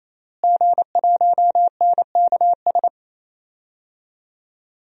Morse code, heard as a keyed tone, about 28 wpm, 700 Hz. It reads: G1NKS
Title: DXCC-100 Call Signs - 5 Characters - Repeated 3x - 28wpm